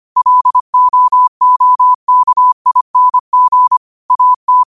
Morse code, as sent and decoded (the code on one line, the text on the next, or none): .-.. --- --- -.- .. -. --. / .- -
LOOKING AT